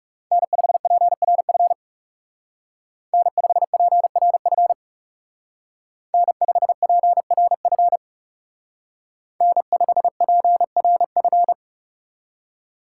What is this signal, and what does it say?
Morse code, heard as a keyed tone, about 35 wpm, 700 Hz. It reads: N5PRF N5PRF N5PRF N5PRF
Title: Speed Racing - USA 1x3 Call Signs 30wpm